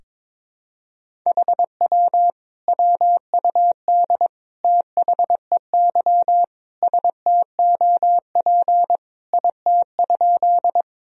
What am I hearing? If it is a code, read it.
Morse code, heard as a keyed tone, about 22 wpm, 700 Hz. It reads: HW WUD THEY STOP IT?
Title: Sentences from Top 300 Words and Abbreviations 22wpm